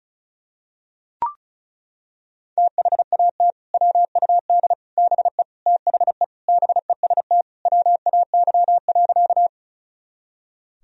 Morse code, heard as a keyed tone, about 34 wpm, 700 Hz. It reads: THAT WUD BE THE BEST WAY.
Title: Sentences from Top 400 Words and Abbreviations 35wpm